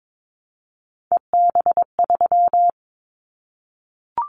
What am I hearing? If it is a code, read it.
Morse code, heard as a keyed tone, about 22 wpm, 700 Hz. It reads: EB3
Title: Callsign Prefixes for Top 25 DXCC - Repeated 3x - 22wpm